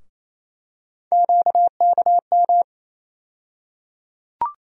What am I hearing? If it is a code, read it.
Morse code, heard as a keyed tone, about 28 wpm, 700 Hz. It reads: QKM
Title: USA Call Sign Postfixes - 3 Letters 28wpm